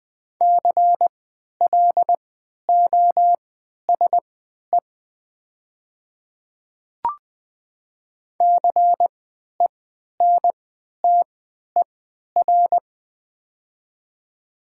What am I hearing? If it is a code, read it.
Morse code, heard as a keyed tone, about 20 wpm, spaced as for 11 wpm, 700 Hz. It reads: CLOSE CENTER
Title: 3x Character Spacing - Top 500 Words - 20wpm